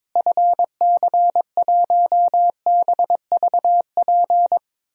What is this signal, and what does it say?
Morse code, heard as a keyed tone, about 22 wpm, 700 Hz. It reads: FC1BVP
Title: DXCC-100 Call Signs - 6 Characters - Repeated 3x - 22wpm